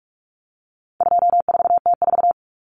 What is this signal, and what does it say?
Morse code, heard as a keyed tone, about 44 wpm, 700 Hz. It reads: 24T4
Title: Common Cut Numbers - 4-Digit 45wpm